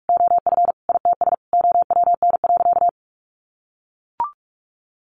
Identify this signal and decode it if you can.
Morse code, heard as a keyed tone, about 44 wpm, 700 Hz. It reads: OF ITS OWN.
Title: Sentences from Top 1200 Words 45wpm